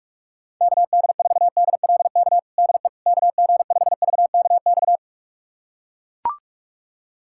Morse code, heard as a keyed tone, about 41 wpm, 700 Hz. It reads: KD4DLK DE KG5VKX
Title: USA Callsign DE Callsign - 6 Characters - 45wpm